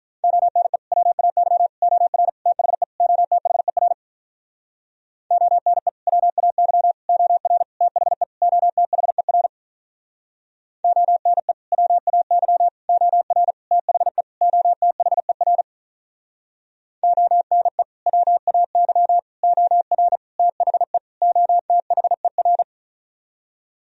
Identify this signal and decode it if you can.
Morse code, heard as a keyed tone, about 46 wpm, 700 Hz. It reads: ONE WAY OR THE OTHER ONE WAY OR THE OTHER ONE WAY OR THE OTHER ONE WAY OR THE OTHER